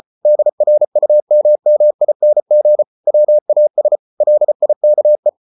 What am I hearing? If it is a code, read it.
Morse code, heard as a keyed tone, about 34 wpm, 600 Hz. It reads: DRUMMING WAS LIKE